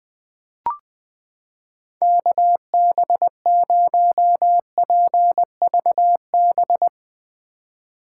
Morse code, heard as a keyed tone, about 21 wpm, 700 Hz. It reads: KB0PVB
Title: 2023 POTA Activator Call Signs - 20wpm